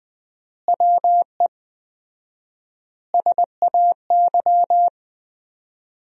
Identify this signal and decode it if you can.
Morse code, heard as a keyed tone, about 20 wpm, 700 Hz. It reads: WE SAY
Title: 4x Word-Spacing - Sentences from Top 200 Words - 20wpm